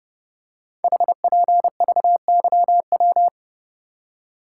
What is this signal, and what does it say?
Morse code, heard as a keyed tone, about 29 wpm, 700 Hz. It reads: HPVYW
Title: Code Groups - 5 Letters - 30wpm